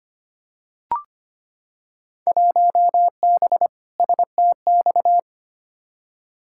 Morse code, heard as a keyed tone, about 25 wpm, 700 Hz. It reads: 1B STX
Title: Field Day Exchanges Encore - 25wpm